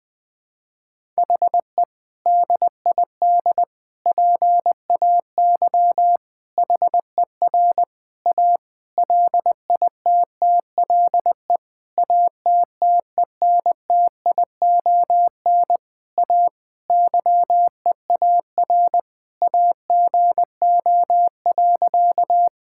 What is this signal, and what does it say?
Morse code, heard as a keyed tone, about 20 wpm, 700 Hz. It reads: HE DID PAY HER A LITTLE ATTENTION A YEAR AGO.
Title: Sentences from Top 1000 Words - Review Part 6 - 20wpm